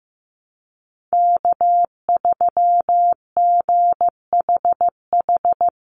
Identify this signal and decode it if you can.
Morse code, heard as a keyed tone, about 15 wpm, 700 Hz. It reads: K3GHH
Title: Call Signs Repeated 3x - 15wpm